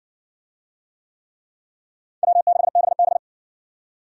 Morse code, heard as a keyed tone, about 60 wpm, 700 Hz. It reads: W6BB